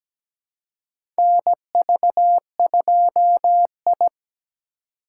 Morse code, heard as a keyed tone, about 17 wpm, 700 Hz. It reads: NV2I
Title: USA Call Signs - 4x Character Spacing - 17wpm